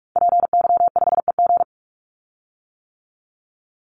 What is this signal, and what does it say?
Morse code, heard as a keyed tone, about 44 wpm, 700 Hz. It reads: PY5EG
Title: Call Signs - 45wpm